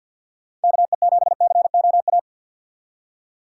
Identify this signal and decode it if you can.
Morse code, heard as a keyed tone, about 50 wpm, 700 Hz. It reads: KE7COA
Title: USA Call Signs - 6 Characters - 4x Character Spacing - 50wpm